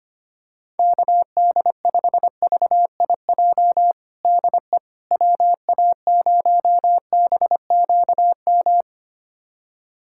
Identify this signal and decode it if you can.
Morse code, heard as a keyed tone, about 25 wpm, 700 Hz. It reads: KD5VIJ DE WA0BQM
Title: USA Callsign DE Callsign - 6 Characters - 25wpm